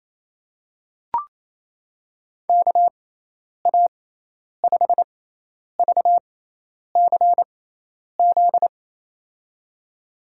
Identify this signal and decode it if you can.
Morse code, heard as a keyed tone, about 28 wpm, spaced as for 10 wpm, 700 Hz. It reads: KA5VCZ
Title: USA Call Signs - 6x Character Spacing - 28wpm